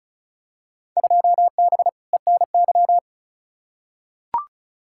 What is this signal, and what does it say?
Morse code, heard as a keyed tone, about 36 wpm, 700 Hz. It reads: 2B ENY
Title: Field Day Exchanges Encore - 35wpm